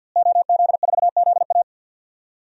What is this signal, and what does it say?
Morse code, heard as a keyed tone, about 52 wpm, 700 Hz. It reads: OZ4ZA
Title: DXCC-100 Call Signs - 5 Characters - Repeated 2x - 50wpm